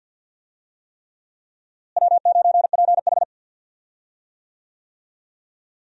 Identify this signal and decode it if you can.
Morse code, heard as a keyed tone, about 48 wpm, 700 Hz. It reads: W9PH
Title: USA Call Signs - 4 Characters - Repeated 3x - 50wpm